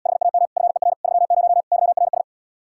Morse code, heard as a keyed tone, about 77 wpm, 700 Hz. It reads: HAM FER 30 YRS